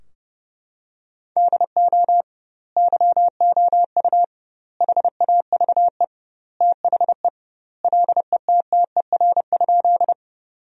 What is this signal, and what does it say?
Morse code, heard as a keyed tone, about 32 wpm, 700 Hz. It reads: DO YOU HAVE THE LETTER?
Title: 2x Word-Spacing - Sentences from Top 300 Words - 30wpm